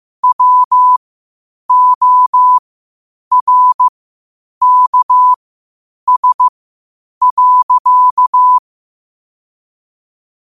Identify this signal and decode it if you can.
Morse code, heard as a keyed tone, about 15 wpm, spaced as for 8 wpm, 1000 Hz. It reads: WORKS.
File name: qotdfiles / 8wpm